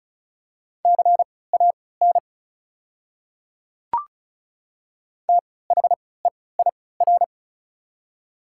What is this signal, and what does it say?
Morse code, heard as a keyed tone, about 34 wpm, spaced as for 20 wpm, 690 Hz. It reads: CAN THEIR
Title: 3x Character Spacing - Top 100 Words - 35wpm